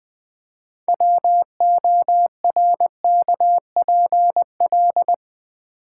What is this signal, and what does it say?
Morse code, heard as a keyed tone, about 20 wpm, 700 Hz. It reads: WORKPL
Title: Build A Word - 3 Letters At A Time - 9-Letter Words - 20wpm